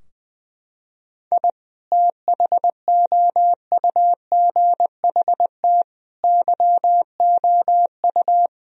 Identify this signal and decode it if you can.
Morse code, heard as a keyed tone, about 20 wpm, 700 Hz. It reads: I THOUGHT YOU
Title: Sentences from Top 1000 Words - Review Part 4 - 20wpm